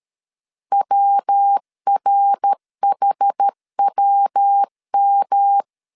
Morse code, heard as a keyed tone, about 13 wpm, 800 Hz. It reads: WRHWM
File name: CW / DET / 1 3 1500C S C 12wpm 2750